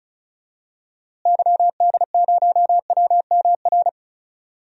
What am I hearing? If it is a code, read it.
Morse code, heard as a keyed tone, about 36 wpm, 700 Hz. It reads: YD0WMR
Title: DXCC-100 Call Signs - 6 Characters - Repeated 3x - 35wpm